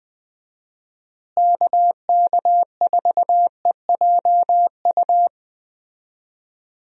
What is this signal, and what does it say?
Morse code, heard as a keyed tone, about 20 wpm, 700 Hz. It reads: KK4EJU
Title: USA Call Signs - 6 Characters - Repeated 3x - 20wpm